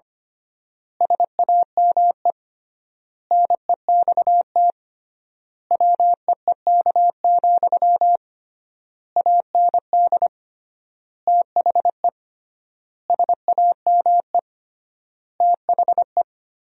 Morse code, heard as a keyed tone, about 25 wpm, 700 Hz. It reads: SAME NEXT WEEK, AND THE SAME THE
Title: 3x Word-Spacing - Sentences from Top 500 Words - 25wpm